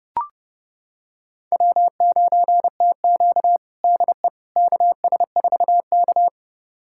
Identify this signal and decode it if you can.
Morse code, heard as a keyed tone, about 29 wpm, 700 Hz. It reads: W9TQ DE KS4K
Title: USA Callsign DE Callsign - 4 Characters - 30wpm